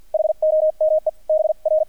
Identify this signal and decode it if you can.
Morse code, heard as a keyed tone, about 48 wpm, 620 Hz. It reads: HOME B R
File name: archived / CRSnet